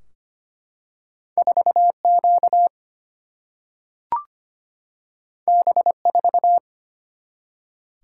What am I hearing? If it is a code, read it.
Morse code, heard as a keyed tone, about 25 wpm, 700 Hz. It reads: 4Q B4
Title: Single-Letter Single-Number 25wpm